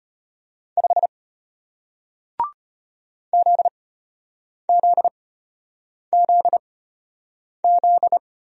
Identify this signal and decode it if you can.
Morse code, heard as a keyed tone, about 33 wpm, spaced as for 8 wpm, 700 Hz. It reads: 5 ZZZZ